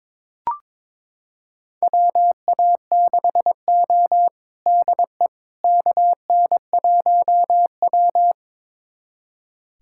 Morse code, heard as a keyed tone, about 22 wpm, 700 Hz. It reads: WA6O DE KN1W
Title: USA Callsign DE Callsign - 4 Characters - 22wpm